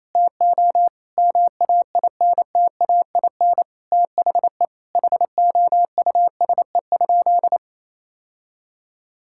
Morse code, heard as a keyed tone, about 28 wpm, 700 Hz. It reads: TO MAINTAIN THE HOUSE?
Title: Sentences from Top 1000 Words - Review Part 5 - 28wpm